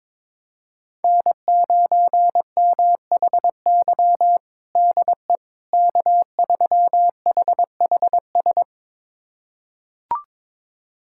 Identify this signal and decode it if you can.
Morse code, heard as a keyed tone, about 22 wpm, 700 Hz. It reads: N9MHY DE K3HHS
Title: USA Callsign DE Callsign - 5 Characters - 22wpm